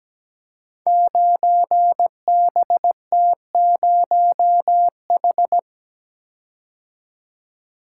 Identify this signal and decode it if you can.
Morse code, heard as a keyed tone, about 17 wpm, 700 Hz. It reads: 9BT0H
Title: Code Groups - 5 Letter:Numbers - 17wpm